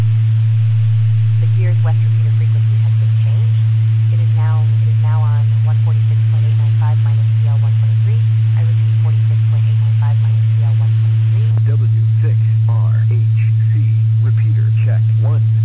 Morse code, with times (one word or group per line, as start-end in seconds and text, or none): none